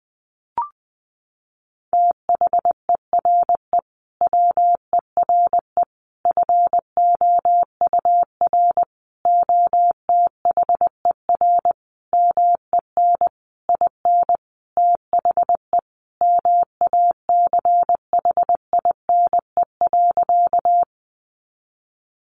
Morse code, heard as a keyed tone, about 20 wpm, 700 Hz. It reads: THERE WERE FOUR OTHER MEN IN THE MACHINE.
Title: A Sound Of Thunder 20wpm